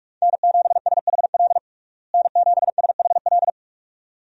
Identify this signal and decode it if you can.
Morse code, heard as a keyed tone, about 46 wpm, 700 Hz. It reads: N7SHL N7SHL
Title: USA Call Signs - 5 Characters - Repeated 2x - 45wpm